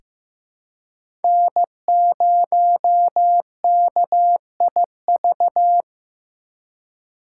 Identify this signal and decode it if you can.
Morse code, heard as a keyed tone, about 15 wpm, 700 Hz. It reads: N0KIV
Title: USA Call Signs - 5 Characters - 4x Character Spacing - 15wpm